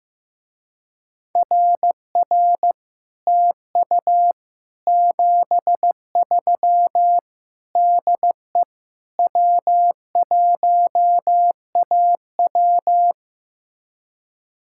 Morse code, heard as a keyed tone, about 15 wpm, 700 Hz. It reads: RR TU 73 DE W1AW
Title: Speed-Racing Common QSO Phrases Encore - 15wpm